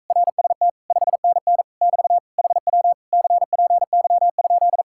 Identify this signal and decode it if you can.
Morse code, heard as a keyed tone, about 45 wpm, 700 Hz. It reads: RST 5NN <BT> HW CPY?